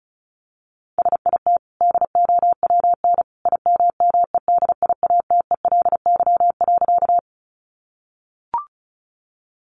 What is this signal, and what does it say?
Morse code, heard as a keyed tone, about 34 wpm, 700 Hz. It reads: SIT DOWN IMMEDIATELY.
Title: Sentences from Top 1100 Words 35wpm